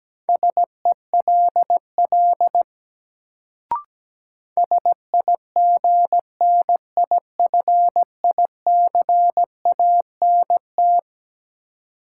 Morse code, heard as a keyed tone, about 17 wpm, 700 Hz. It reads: SELL SIGNIFICANT
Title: Sentences from Top 1000 Words - Review Part 6 - New Word Drill - 17wpm